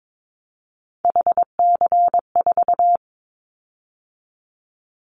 Morse code, heard as a keyed tone, about 22 wpm, 700 Hz. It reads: HC4
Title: Callsign Prefixes for Top 100 DXCC - 22wpm